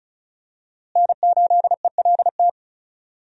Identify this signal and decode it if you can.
Morse code, heard as a keyed tone, about 34 wpm, 700 Hz. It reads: N8ELT
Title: USA Call Signs - 5 Characters - Repeated 3x - 35wpm